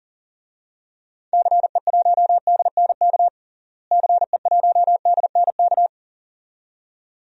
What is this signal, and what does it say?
Morse code, heard as a keyed tone, about 37 wpm, 700 Hz. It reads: CE1DNK CE1DNK
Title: DXCC-100 Call Signs - 6 Characters - Repeated 2x - 40wpm